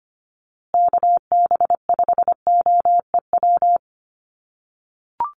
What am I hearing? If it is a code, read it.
Morse code, heard as a keyed tone, about 25 wpm, 700 Hz. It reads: KB5OEW
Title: USA Call Signs Repeated 3x 25wpm